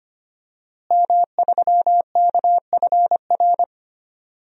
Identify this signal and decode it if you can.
Morse code, heard as a keyed tone, about 25 wpm, 700 Hz. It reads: M3KFR
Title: DXCC-100 Call Signs - 5 Characters - Repeated 2x - 25wpm